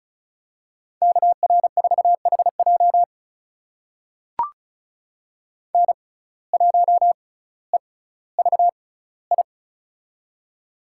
Morse code, heard as a keyed tone, about 36 wpm, spaced as for 18 wpm, 700 Hz. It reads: KR4HJ N1EVI
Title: USA Call Signs - 5 Characters - 6x Character Spacing - 35wpm